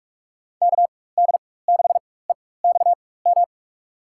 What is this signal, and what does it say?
Morse code, heard as a keyed tone, about 44 wpm, spaced as for 21 wpm, 700 Hz. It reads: KD6EXM
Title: USA Call Signs - 6 Characters - 4x Character Spacing - 45wpm